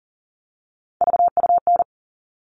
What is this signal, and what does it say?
Morse code, heard as a keyed tone, about 40 wpm, 700 Hz. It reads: VUN
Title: All Cut Numbers - 3-Digit 40wpm